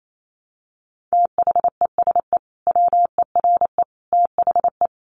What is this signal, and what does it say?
Morse code, heard as a keyed tone, about 28 wpm, 700 Hz. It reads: THESE WERE THE
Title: Sentences from Top 300 Words 28wpm